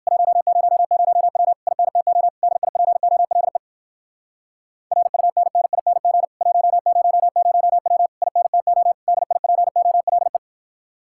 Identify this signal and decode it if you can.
Morse code, heard as a keyed tone, about 56 wpm, 700 Hz. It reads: 100W INTO DIPOLE RUNNING 100W INTO DIPOLE
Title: Speed-Racing Common QSO Phrases - 45wpm